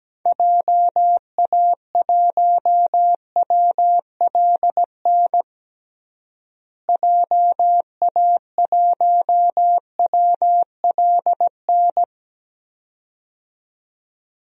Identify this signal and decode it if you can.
Morse code, heard as a keyed tone, about 17 wpm, 700 Hz. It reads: JA1WLN JA1WLN
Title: DXCC-100 Call Signs - 6 Characters - Repeated 2x - 17wpm